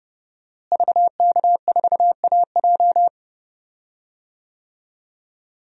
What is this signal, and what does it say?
Morse code, heard as a keyed tone, about 31 wpm, 700 Hz. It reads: VK4AJ
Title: DXCC-100 Call Signs - 5 Characters - Repeated 2x - 30wpm